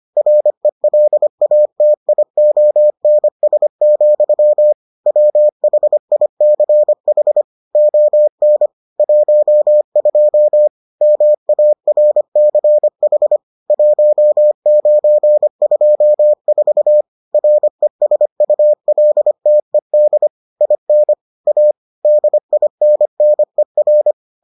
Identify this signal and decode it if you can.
Morse code, heard as a keyed tone, about 25 wpm, 600 Hz. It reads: RELATIONS, WHICH ON 12 MARCH 1924 RESULTED IN A DINNER